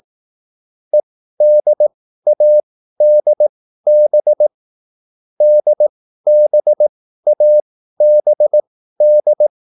Morse code, heard as a keyed tone, about 18 wpm, spaced as for 13 wpm, 600 Hz. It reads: EDADB DBABD
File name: QSL / CW lesson